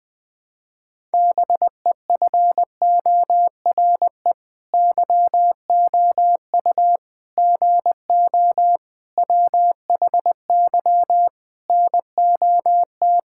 Morse code, heard as a keyed tone, about 20 wpm, 700 Hz. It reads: BEFORE YOU GO WHY NOT
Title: Sets of 9 Words - 20wpm